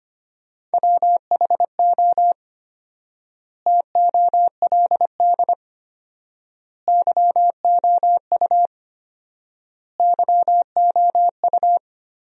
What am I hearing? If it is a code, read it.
Morse code, heard as a keyed tone, about 25 wpm, 700 Hz. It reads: WHO TOLD YOU YOU
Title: 4x Word-Spacing - Sentences from Top 400 Words - 25wpm